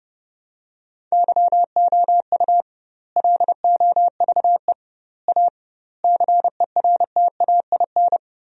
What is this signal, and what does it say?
Morse code, heard as a keyed tone, about 30 wpm, 700 Hz. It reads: YOU LOVE A CERTAIN
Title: 2x Word-Spacing - Sentences from Top 500 Words - 30wpm